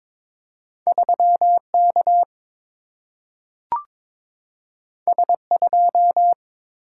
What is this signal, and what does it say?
Morse code, heard as a keyed tone, about 22 wpm, 700 Hz. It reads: 3 K S 2
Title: Single-Letter Single-Number 22wpm